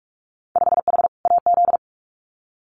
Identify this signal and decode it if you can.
Morse code, heard as a keyed tone, about 46 wpm, 700 Hz. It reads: 5H AZ
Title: Winter Field Day Exchanges Encore 45wpm